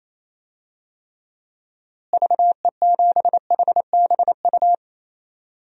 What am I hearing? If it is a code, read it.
Morse code, heard as a keyed tone, about 28 wpm, 700 Hz. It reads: VE7HBU
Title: DXCC-100 Call Signs - 6 Characters - Repeated 2x - 28wpm